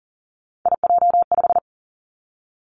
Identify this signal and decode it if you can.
Morse code, heard as a keyed tone, about 42 wpm, 700 Hz. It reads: IJ5
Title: Three Challenging Letter-Number Combinations 40wpm